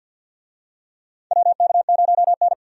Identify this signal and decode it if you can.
Morse code, heard as a keyed tone, about 52 wpm, 700 Hz. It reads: WK0N
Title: USA Call Signs - 4 Characters - 50wpm